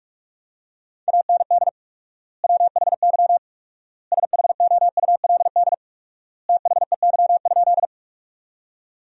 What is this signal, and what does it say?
Morse code, heard as a keyed tone, about 42 wpm, 700 Hz. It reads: AND WHY SHOULD THEY?
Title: 4x Word-Spacing - Sentences from Top 300 Words - 45wpm